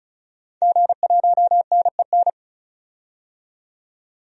Morse code, heard as a keyed tone, about 36 wpm, 700 Hz. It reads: G1NEN
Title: DXCC-100 Call Signs - 5 Characters - 35wpm